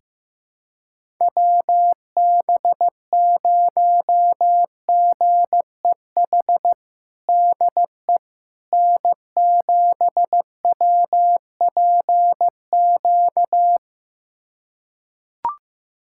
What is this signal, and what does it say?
Morse code, heard as a keyed tone, about 15 wpm, 700 Hz. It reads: WB0GEH DE N7WPQ
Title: USA Callsign DE Callsign 15wpm